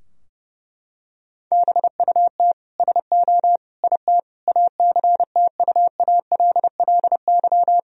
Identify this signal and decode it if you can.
Morse code, heard as a keyed tone, about 30 wpm, 700 Hz. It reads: BUT SO IT ACTUALLY
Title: Sentences from Top 1000 Words - Review Part 4 - 30wpm